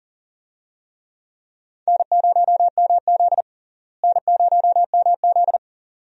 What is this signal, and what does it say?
Morse code, heard as a keyed tone, about 37 wpm, 700 Hz. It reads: N0MZ N0MZ
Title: USA Call Signs - 4 Characters - Repeated 2x - 40wpm